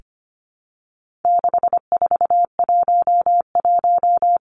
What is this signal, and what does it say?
Morse code, heard as a keyed tone, about 25 wpm, 700 Hz. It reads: 6411
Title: Numbers - 4-Digit 25wpm